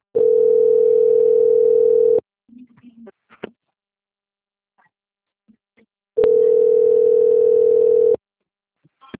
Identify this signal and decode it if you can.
Ringback tone (North America), 440 + 480 Hz.